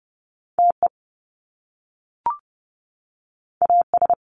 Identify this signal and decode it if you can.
Morse code, heard as a keyed tone, about 30 wpm, 700 Hz. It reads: TE AS
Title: Top Two Letter Combinations in Words 30wpm